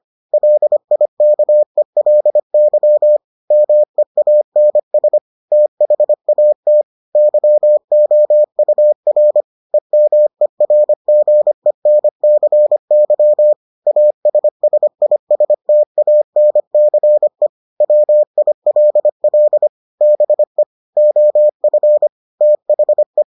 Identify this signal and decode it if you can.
Morse code, heard as a keyed tone, about 25 wpm, 600 Hz. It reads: LIKELY MEANS THAT YOUR EMERGENCY ASSISTANCE WILL BE OF THE